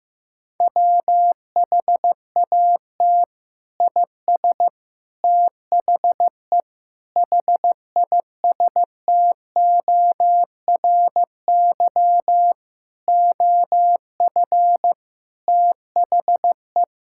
Morse code, heard as a keyed tone, about 15 wpm, 700 Hz. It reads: WHAT IS THE HISTORY OF THE
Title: Sentences from Top 1000 Words - Review Part 5 - 15wpm